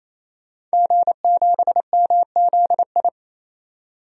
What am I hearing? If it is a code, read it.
Morse code, heard as a keyed tone, about 28 wpm, 700 Hz. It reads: G7MZI
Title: DXCC-100 Call Signs - 5 Characters - 28wpm